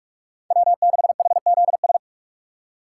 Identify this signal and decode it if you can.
Morse code, heard as a keyed tone, about 46 wpm, 700 Hz. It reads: W6HZS
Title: USA Call Signs - 5 Characters - 2x Character Spacing - 45wpm